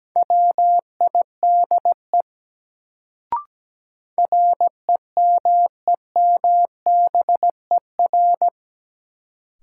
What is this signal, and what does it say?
Morse code, heard as a keyed tone, about 17 wpm, 700 Hz. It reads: WIDE REMEMBER
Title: Sets of 4 Words - New Word Drill - 17wpm